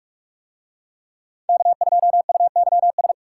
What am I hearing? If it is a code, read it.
Morse code, heard as a keyed tone, about 44 wpm, 700 Hz. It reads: K2UYS